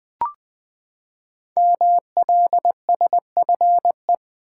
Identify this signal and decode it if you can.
Morse code, heard as a keyed tone, about 20 wpm, 700 Hz. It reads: MLSFE
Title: Code Groups - 5 Letters - 20wpm